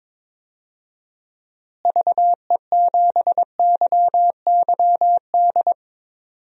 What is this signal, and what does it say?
Morse code, heard as a keyed tone, about 22 wpm, 700 Hz. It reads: VE7YYD
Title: DXCC-100 Call Signs - 6 Characters - Repeated 2x - 22wpm